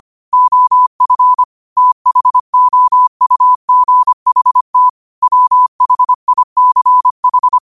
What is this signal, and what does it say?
Morse code, heard as a keyed tone, about 25 wpm, 1000 Hz. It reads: OF THOUGHT WHICH
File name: qotdfiles / 25wpm